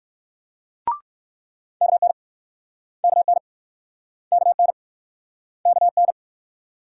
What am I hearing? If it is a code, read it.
Morse code, heard as a keyed tone, about 60 wpm, 690 Hz. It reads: KN KN KN KN